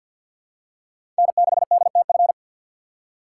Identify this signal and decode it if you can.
Morse code, heard as a keyed tone, about 51 wpm, 700 Hz. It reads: N6DTF
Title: USA Call Signs - 2x Character Spacing - 50wpm